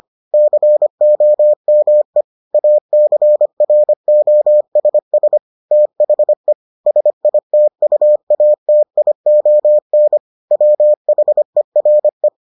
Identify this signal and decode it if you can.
Morse code, heard as a keyed tone, about 25 wpm, 600 Hz. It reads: COME ACROSS THE SITUATION WHERE